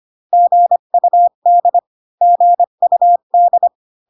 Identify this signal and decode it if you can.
Morse code, heard as a keyed tone, about 25 wpm, 700 Hz. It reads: GUD GUD